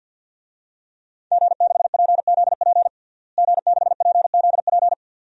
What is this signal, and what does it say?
Morse code, heard as a keyed tone, about 47 wpm, 700 Hz. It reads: G6PZP G6PZP
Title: DXCC-100 Call Signs - 5 Characters - Repeated 2x - 50wpm